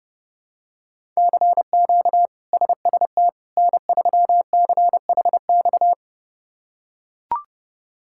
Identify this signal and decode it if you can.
Morse code, heard as a keyed tone, about 32 wpm, 700 Hz. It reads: CQ SST N3CHX
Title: K1USN SST QSO Practice - 30wpm